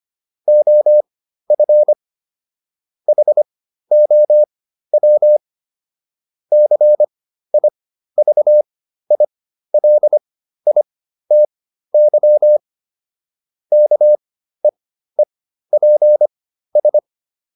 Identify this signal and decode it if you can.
Morse code, heard as a keyed tone, about 25 wpm, spaced as for 13 wpm, 600 Hz. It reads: OF HOW CIVILITY KEEPS